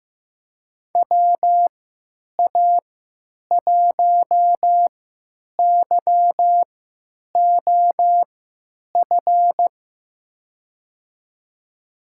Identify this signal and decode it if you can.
Morse code, heard as a keyed tone, about 15 wpm, spaced as for 9 wpm, 700 Hz. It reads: WA1YOF
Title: USA Call Signs - 6 Characters - 3x Character Spacing - 15wpm